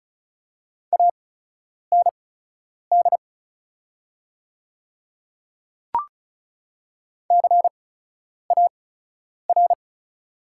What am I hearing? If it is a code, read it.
Morse code, heard as a keyed tone, about 34 wpm, spaced as for 10 wpm, 700 Hz. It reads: AND CAR
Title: 8x Character Spacing - Top 500 Words - 35wpm